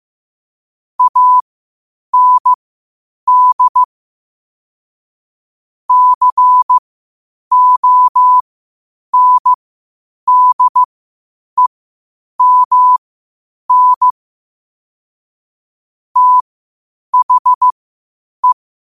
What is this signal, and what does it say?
Morse code, heard as a keyed tone, about 15 wpm, spaced as for 8 wpm, 1000 Hz. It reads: AND CONDEMN THE
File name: qotdfiles / 8wpm